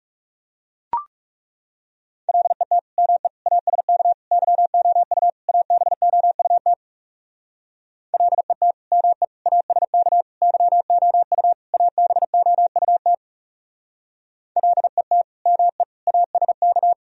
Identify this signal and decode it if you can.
Morse code, heard as a keyed tone, about 38 wpm, 700 Hz. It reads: LET ME ASK YOU ABOUT LET ME ASK YOU ABOUT LET ME ASK